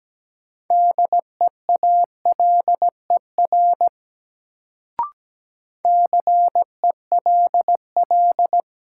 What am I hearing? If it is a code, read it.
Morse code, heard as a keyed tone, about 17 wpm, 700 Hz. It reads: DEALER CELL
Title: Sentences from Top 1100 Words - New Word Drill Alt - 17wpm